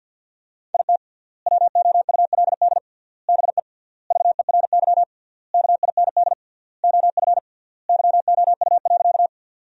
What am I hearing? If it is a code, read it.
Morse code, heard as a keyed tone, about 50 wpm, 700 Hz. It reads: IT WOULD BE VERY KIND OF YOU.
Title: 3x Word-Spacing - Sentences from Top 200 Words - 50wpm